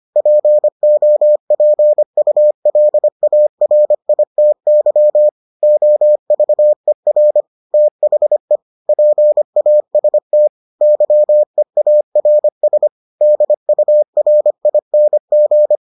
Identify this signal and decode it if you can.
Morse code, heard as a keyed tone, about 25 wpm, 600 Hz. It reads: POPULARITY OVER THE PAST YEARS DURING